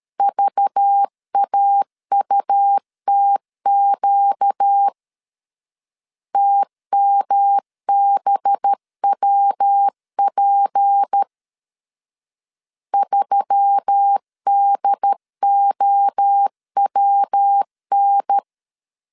Morse code, heard as a keyed tone, about 13 wpm, 800 Hz. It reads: VAUTQ TMBWP 3DOWN
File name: LEZIONI / 6 PKX0 / 6 P 12wpm 2033